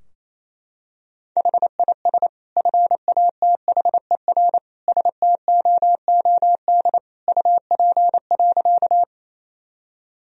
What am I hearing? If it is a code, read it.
Morse code, heard as a keyed tone, about 28 wpm, 700 Hz. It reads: HIS FATHER STOOD UP.
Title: Sentences from Top 1000 Words - Review Part 2 - 28wpm